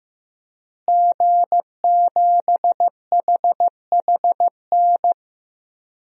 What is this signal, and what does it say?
Morse code, heard as a keyed tone, about 15 wpm, 700 Hz. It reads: G7HHN
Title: DXCC-100 Call Signs - 5 Characters - Repeated 2x - 15wpm